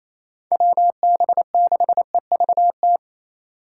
Morse code, heard as a keyed tone, about 35 wpm, 700 Hz. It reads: WB6EVT
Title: Speed Racing - USA 2x3 Call Signs 28wpm